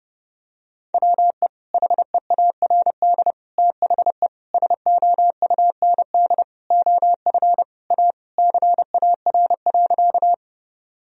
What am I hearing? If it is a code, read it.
Morse code, heard as a keyed tone, about 30 wpm, 700 Hz. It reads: WE HEARD THE SOUND OF A CAR.
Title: Sentences from Top 1000 Words - Review Part 1 - 30wpm